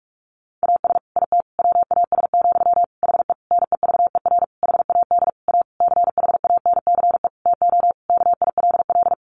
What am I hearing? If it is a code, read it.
Morse code, heard as a keyed tone, about 44 wpm, 700 Hz. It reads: AS IT WAS, HE NEVER HAD A CHANCE TO KILL
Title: Sentences from Top 1400 Words 45wpm